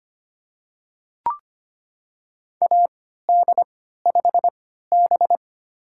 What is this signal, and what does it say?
Morse code, heard as a keyed tone, about 25 wpm, spaced as for 14 wpm, 700 Hz. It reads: AD5B
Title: USA Call Signs - 4 Characters - 3x Character Spacing - 25wpm